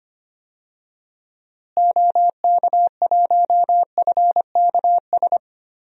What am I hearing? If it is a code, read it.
Morse code, heard as a keyed tone, about 25 wpm, 700 Hz. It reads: OK1FKS
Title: DXCC-100 Call Signs - 6 Characters - Repeated 2x - 25wpm